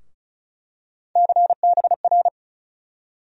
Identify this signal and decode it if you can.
Morse code, heard as a keyed tone, about 35 wpm, 700 Hz. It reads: CBR
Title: USA Call Sign Postfixes - 3 Letters 35wpm